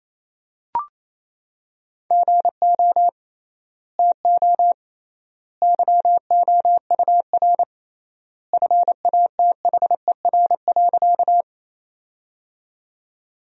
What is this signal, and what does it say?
Morse code, heard as a keyed tone, about 28 wpm, 700 Hz. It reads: GO TO YOUR FATHER.
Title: 3x Word-Spacing - Sentences from Top 300 Words - 28wpm